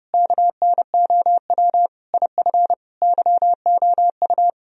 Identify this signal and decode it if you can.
Morse code, heard as a keyed tone, about 36 wpm, 700 Hz. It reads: KNOW IF YOU
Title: Speed-Racing - Sets of 5 Words 30wpm